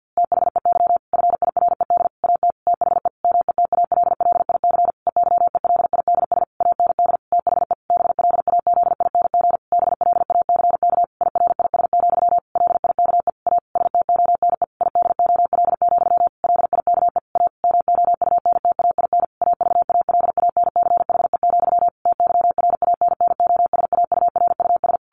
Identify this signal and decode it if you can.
Morse code, heard as a keyed tone, about 50 wpm, 700 Hz. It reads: THEY FIRED AT THE METALLIC EYELIDS AND THE BLAZING BLACK IRIS, LIKE A STONE IDOL, LIKE A MOUNTAIN AVALANCHE, TYRANNOSAURUS
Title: A Sound Of Thunder 50wpm